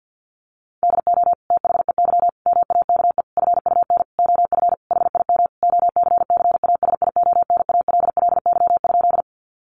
Morse code, heard as a keyed tone, about 50 wpm, 700 Hz. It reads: DO THEY MAKE FUN OF HIM OCCASIONALLY?
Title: Sentences from Top 1300 Words 50wpm